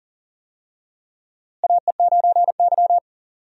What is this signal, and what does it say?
Morse code, heard as a keyed tone, about 37 wpm, 700 Hz. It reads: AE9Y
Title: USA Call Signs - 4 Characters - Repeated 2x - 40wpm